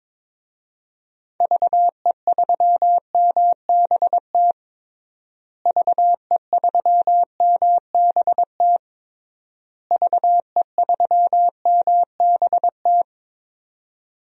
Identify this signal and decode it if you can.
Morse code, heard as a keyed tone, about 22 wpm, 700 Hz. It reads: VE3MBT VE3MBT VE3MBT
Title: DXCC-100 Call Signs - 6 Characters - Repeated 3x - 22wpm